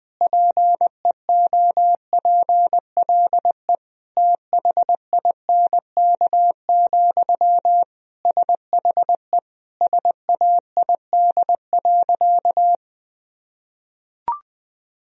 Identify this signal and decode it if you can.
Morse code, heard as a keyed tone, about 20 wpm, 700 Hz. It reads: PEOPLE THINK, SHE SAID.
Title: Sentences from Top 1000 Words - Review Part 4 - 20wpm